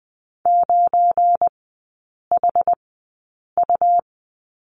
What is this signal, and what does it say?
Morse code, heard as a keyed tone, about 20 wpm, spaced as for 8 wpm, 700 Hz. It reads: 9HU
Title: 205 Lesson 33e - Letter j - Three Characters 20wpm